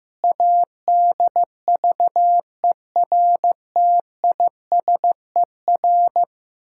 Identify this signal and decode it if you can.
Morse code, heard as a keyed tone, about 15 wpm, 700 Hz. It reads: ADVERTISER
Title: Pangram Challenge v2 - 15wpm